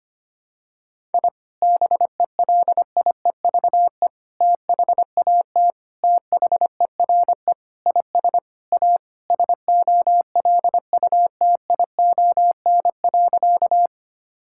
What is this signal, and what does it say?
Morse code, heard as a keyed tone, about 25 wpm, 700 Hz. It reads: I BELIEVE THAT THERE IS A SOLUTION.
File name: practice / 25wpm / Sentences with 4-Word Phrases - 25wpm